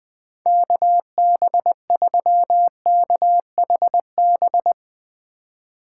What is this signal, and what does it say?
Morse code, heard as a keyed tone, about 22 wpm, 700 Hz. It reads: KB3KHB
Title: Speed Racing - USA Call Signs 20wpm